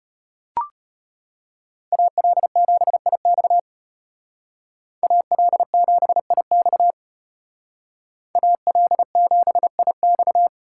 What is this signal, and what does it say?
Morse code, heard as a keyed tone, about 32 wpm, 700 Hz. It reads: AL7IX AL7IX AL7IX